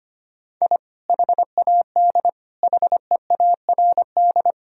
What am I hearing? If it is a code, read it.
Morse code, heard as a keyed tone, about 25 wpm, 700 Hz. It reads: I HAD HEARD
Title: Sentences from Top 1000 Words - Review Part 1 - 25wpm